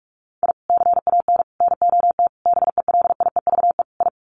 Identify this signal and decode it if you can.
Morse code, heard as a keyed tone, about 44 wpm, 700 Hz. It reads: I CAN NOT BELIEVE I
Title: Sets of 5 Words Encore2 45wpm